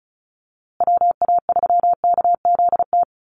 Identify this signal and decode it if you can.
Morse code, heard as a keyed tone, about 34 wpm, 700 Hz. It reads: WA3KZT
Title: USA Call Signs Repeated 3x 35wpm